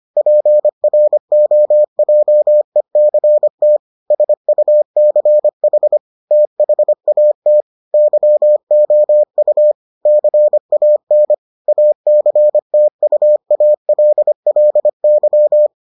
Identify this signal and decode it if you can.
Morse code, heard as a keyed tone, about 25 wpm, 600 Hz. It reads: PROJECT SUCH THAT YOU CAN ACTUALLY